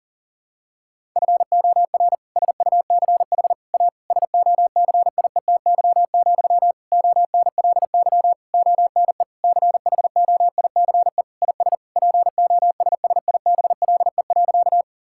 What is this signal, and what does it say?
Morse code, heard as a keyed tone, about 40 wpm, 700 Hz. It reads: FOR SUCH A SOCIETY, ONLY ONE CHOICE IS POSSIBLE.
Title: Sentences from Top 1000 Words - Review Part 6 - 40wpm